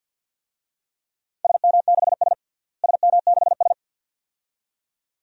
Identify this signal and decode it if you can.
Morse code, heard as a keyed tone, about 53 wpm, 700 Hz. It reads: SM6S SM6S
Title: DXCC-100 Call Signs - 4 Characters - Repeated 2x - 50wpm